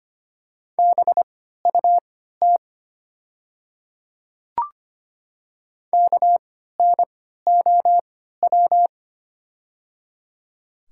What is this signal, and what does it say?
Morse code, heard as a keyed tone, about 25 wpm, spaced as for 14 wpm, 690 Hz. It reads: BUT KNOW
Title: 3x Character Spacing - Top 300 Words - 25wpm